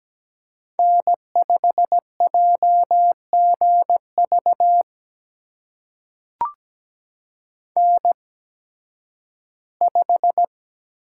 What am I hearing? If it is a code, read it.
Morse code, heard as a keyed tone, about 17 wpm, spaced as for 11 wpm, 700 Hz. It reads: N5JGV N5
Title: USA Call Signs - 5 Characters - 8x Character Spacing - 17wpm